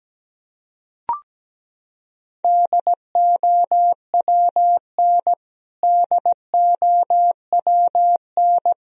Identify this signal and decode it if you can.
Morse code, heard as a keyed tone, about 17 wpm, 700 Hz. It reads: DOWN DOWN
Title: Top 100 Words Repeated 17wpm